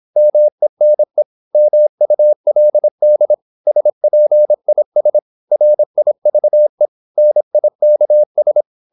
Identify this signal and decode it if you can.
Morse code, heard as a keyed tone, about 26 wpm, 600 Hz. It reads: MENE MULD SPIS RIVE NIKS
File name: Korte ord DK 26wpm